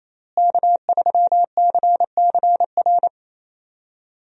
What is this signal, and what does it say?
Morse code, heard as a keyed tone, about 28 wpm, 700 Hz. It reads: K3CCR
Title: USA Call Signs - 5 Characters - Repeated 3x - 28wpm